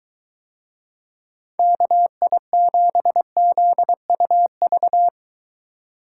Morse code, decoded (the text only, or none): KI7ZUV